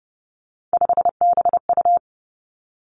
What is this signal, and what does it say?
Morse code, heard as a keyed tone, about 30 wpm, 700 Hz. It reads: 5BU
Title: Three Challenging Letter-Number Combinations 30wpm